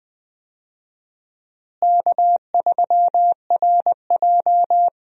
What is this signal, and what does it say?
Morse code, heard as a keyed tone, about 20 wpm, 700 Hz. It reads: K3RJ
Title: USA Call Signs - 4 Characters - Repeated 3x - 20wpm